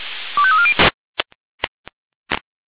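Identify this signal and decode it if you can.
Five-tone selective call (ZVEI), heard as quick runs of five tones, tones of about 70 ms. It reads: 25340
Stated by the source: ZVEI five-tone selective call 25339 received on SDR.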